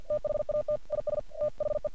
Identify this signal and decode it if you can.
Morse code, heard as a keyed tone, about 50 wpm, 600 Hz. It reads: THAT IS THE
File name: archived / CRSnet